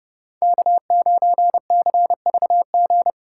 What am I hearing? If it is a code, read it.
Morse code, heard as a keyed tone, about 39 wpm, 700 Hz. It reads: K9CVG